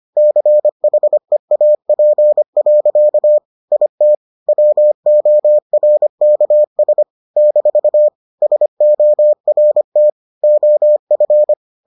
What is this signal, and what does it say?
Morse code, heard as a keyed tone, about 25 wpm, 600 Hz. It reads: CHEAP. IT WORKS - SORT OF